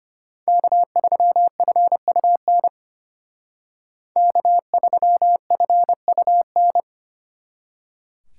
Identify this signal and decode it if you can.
Morse code, heard as a keyed tone, about 31 wpm, 700 Hz. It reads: K3FUN K3FUN